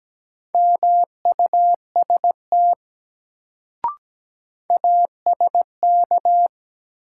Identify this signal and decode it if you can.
Morse code, heard as a keyed tone, about 17 wpm, 700 Hz. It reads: MUST ASK
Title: Sentences from Top 200 Words - New Word Drill - 17wpm